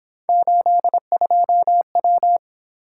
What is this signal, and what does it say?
Morse code, heard as a keyed tone, about 26 wpm, 700 Hz. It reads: 82W
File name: Speed-Racing - Three Letter-Number Combinations - 22wpm